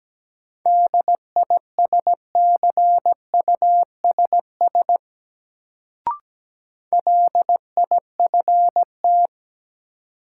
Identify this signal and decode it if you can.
Morse code, heard as a keyed tone, about 17 wpm, 700 Hz. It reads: DISCUSS LIFT
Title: Sentences from Top 1000 Words - New Word Drill - 17wpm